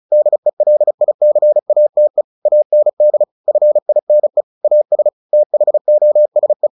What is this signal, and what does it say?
Morse code, heard as a keyed tone, about 34 wpm, 600 Hz. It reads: DELICATE AND FINE AS THOSE
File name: the old man and the sea / the old man and the sea 1 35wpm